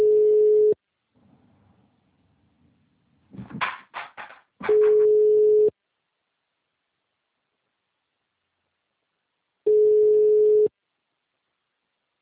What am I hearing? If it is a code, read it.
Ringback tone (Europe, 425 Hz).